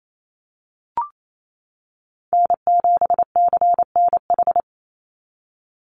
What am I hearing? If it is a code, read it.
Morse code, heard as a keyed tone, about 28 wpm, 700 Hz. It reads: N7CNH
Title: NA SOTA Chaser Callsigns 28wpm